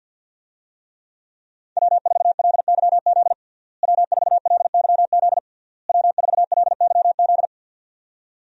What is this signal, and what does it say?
Morse code, heard as a keyed tone, about 52 wpm, 700 Hz. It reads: W4LYZ W4LYZ W4LYZ
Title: USA Call Signs - 5 Characters - Repeated 3x - 50wpm